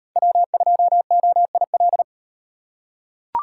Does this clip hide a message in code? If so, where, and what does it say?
Yes: Morse code from 0.2 to 2.0 s, heard as a keyed tone, about 38 wpm, 700 Hz. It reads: W2OIL